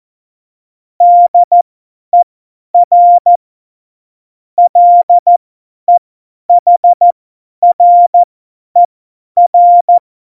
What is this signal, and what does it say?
Morse code, heard as a keyed tone, about 14 wpm, spaced as for 10 wpm, 700 Hz. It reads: DER LEHRER
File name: Morsekurs / Witze / Witz 11. Juni 2019, 50 F70